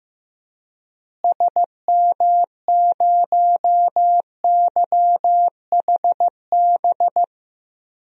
Morse code, heard as a keyed tone, about 15 wpm, 700 Hz. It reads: SM0YHB
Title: DXCC-100 Call Signs - 6 Characters - Repeated 2x - 15wpm